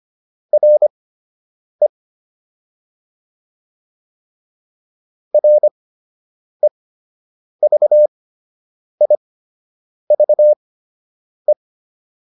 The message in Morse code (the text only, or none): RE REVIVE